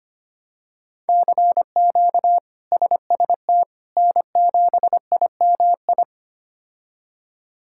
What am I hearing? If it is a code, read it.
Morse code, heard as a keyed tone, about 25 wpm, 700 Hz. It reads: CQ SST N7IMI
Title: K1USN SST QSO Practice - 25wpm